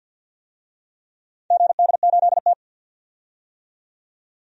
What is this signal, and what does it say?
Morse code, heard as a keyed tone, about 52 wpm, 700 Hz. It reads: GD8T
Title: DXCC-100 Call Signs - 4 Characters - 50wpm